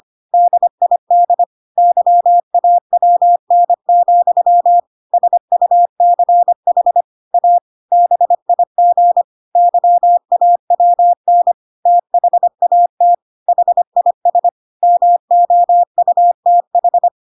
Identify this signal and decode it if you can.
Morse code, heard as a keyed tone, about 25 wpm, 700 Hz. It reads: DID YAWN, SUCH A BIG YAWN THAT HIS MOUTH
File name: Pinocchio / Pinocchio ch5 25 25 ik1hge